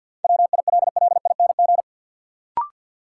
Morse code, heard as a keyed tone, about 46 wpm, 700 Hz. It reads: WILLING
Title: Sentences from Top 1000 Words - New Word Drill - 50wpm